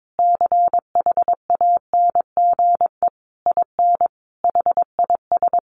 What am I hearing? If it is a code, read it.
Morse code, heard as a keyed tone, about 22 wpm, 700 Hz. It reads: CHANGE IN HIS
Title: Sentences from Top 1200 Words 22wpm